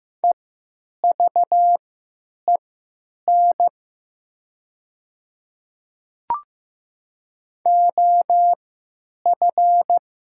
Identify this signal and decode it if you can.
Morse code, heard as a keyed tone, about 16 wpm, spaced as for 9 wpm, 700 Hz. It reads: EVEN OF
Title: 3x Character Spacing - Top 300 Words - 15wpm